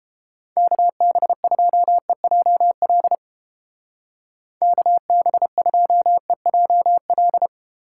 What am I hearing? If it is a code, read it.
Morse code, heard as a keyed tone, about 31 wpm, 700 Hz. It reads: KB2EJL KB2EJL